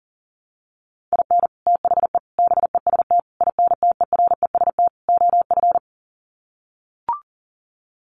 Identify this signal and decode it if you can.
Morse code, heard as a keyed tone, about 40 wpm, 700 Hz. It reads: IN THE BEST INTEREST OF
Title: Sets of 5 Words Encore 40wpm